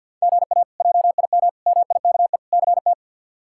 Morse code, heard as a keyed tone, about 49 wpm, 700 Hz. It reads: GA JIM MIKE CT
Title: K1USN SST QSO Practice - 50wpm